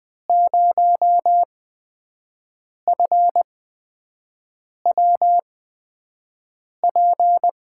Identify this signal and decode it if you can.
Morse code, heard as a keyed tone, about 21 wpm, spaced as for 5 wpm, 700 Hz. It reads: 0FWP
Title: USA Call Signs - 6 Characters - 8x Character Spacing - 20wpm